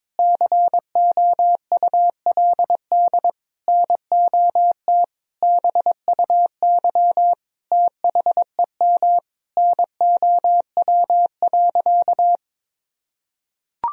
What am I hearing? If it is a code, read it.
Morse code, heard as a keyed tone, about 22 wpm, 700 Hz. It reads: COULD NOT BUY THEM NOW.
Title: Sentences from Top 1000 Words - Review Part 4 - 22wpm